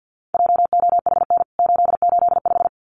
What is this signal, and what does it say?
Morse code, heard as a keyed tone, about 49 wpm, 700 Hz. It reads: JOHN 885
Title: CWT Exchange 50wpm